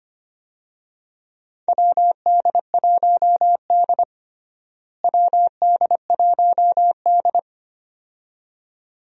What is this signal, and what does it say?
Morse code, heard as a keyed tone, about 25 wpm, 700 Hz. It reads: WD1D WD1D
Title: USA Call Signs - 4 Characters - Repeated 2x - 25wpm